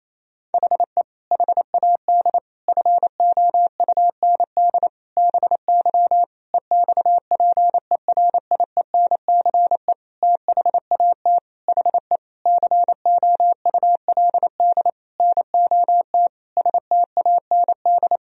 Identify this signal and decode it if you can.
Morse code, heard as a keyed tone, about 27 wpm, 700 Hz. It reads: HE HAD FOUND BY EXPERIENCE THAT HE COULD NOT STAND
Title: Sentences from Top 1000 Words - Review Part 3 - 28wpm